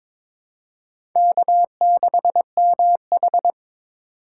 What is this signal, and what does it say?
Morse code, heard as a keyed tone, about 21 wpm, 700 Hz. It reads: K6MH
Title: USA Call Signs - 4 Characters - 4x Character Spacing - 22wpm